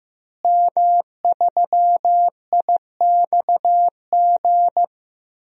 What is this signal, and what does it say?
Morse code, heard as a keyed tone, about 15 wpm, 700 Hz. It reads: M3IXG
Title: DXCC-100 Call Signs - 5 Characters - Repeated 3x - 15wpm